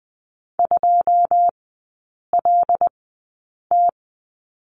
Morse code, heard as a keyed tone, about 20 wpm, spaced as for 8 wpm, 700 Hz. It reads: 2LT
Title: 114 Lesson 19e - Letter w - Three Characters 20wpm